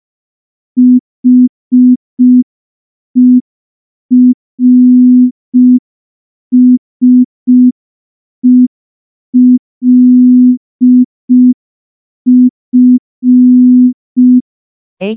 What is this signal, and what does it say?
Morse code, heard as a keyed tone, about 5 wpm, 250 Hz. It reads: HERSELF
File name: morse / morse-training-05WPM